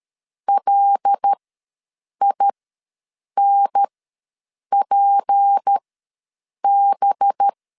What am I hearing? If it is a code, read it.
Morse code, heard as a keyed tone, about 13 wpm, spaced as for 7 wpm, 800 Hz. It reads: LINPB
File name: LEZIONI / 6 PKX0 / 6 PKX ws 12wpm 3106